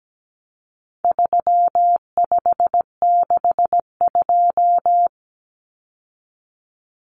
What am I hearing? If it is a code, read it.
Morse code, heard as a keyed tone, about 17 wpm, 700 Hz. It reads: 3562
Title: Numbers - 4-Digit 17wpm